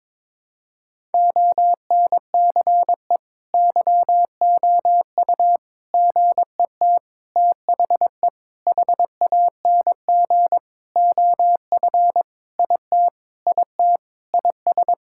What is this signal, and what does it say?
Morse code, heard as a keyed tone, about 21 wpm, 700 Hz. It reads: ONCE YOU GET THE HANG OF IT IT IS
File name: Sentences with 7-Word Phrases - 22wpm